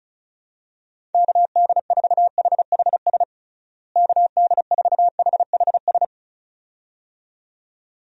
Morse code, heard as a keyed tone, about 36 wpm, 700 Hz. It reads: KD4HHS KD4HHS
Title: USA Call Signs - 6 Characters - Repeated 2x - 35wpm